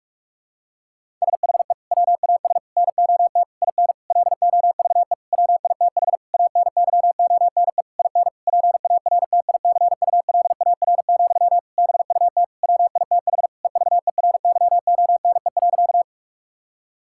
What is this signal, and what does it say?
Morse code, heard as a keyed tone, about 41 wpm, 700 Hz. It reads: SHE WAS NOT IN LOVE WITH ANYONE IN PARTICULAR, BUT WITH EVERYONE.
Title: Sentences from Top 1000 Words - Review Part 5 - 45wpm